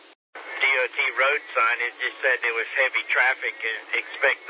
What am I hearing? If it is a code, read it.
Emergency Alert System (SAME) tones, heard as harsh buzzing bursts, with speech. It is heard from 0.3 to 3.6 s.